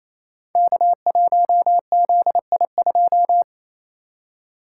Morse code, heard as a keyed tone, about 27 wpm, 700 Hz. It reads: K1ZI2
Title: Code Groups - 5 Letter:Numbers - 28wpm